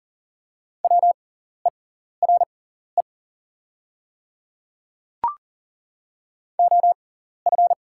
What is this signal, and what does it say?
Morse code, heard as a keyed tone, about 39 wpm, spaced as for 14 wpm, 690 Hz. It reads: WERE OF
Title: 6x Character Spacing - Top 100 Words - 40wpm